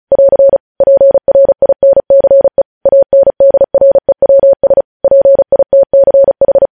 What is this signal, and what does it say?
Morse code, heard as a keyed tone, about 36 wpm, 550 Hz. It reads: <AR> PRINCE ANDREWS PITCH